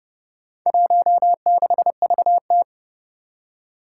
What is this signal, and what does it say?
Morse code, heard as a keyed tone, about 28 wpm, 700 Hz. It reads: 16VT
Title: Four Challenging Letter-Number Combinations 30wpm